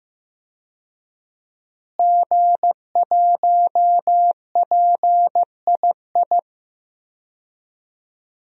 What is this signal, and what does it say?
Morse code, heard as a keyed tone, about 15 wpm, 700 Hz. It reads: G1PII
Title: DXCC-100 Call Signs - 5 Characters - Repeated 3x - 15wpm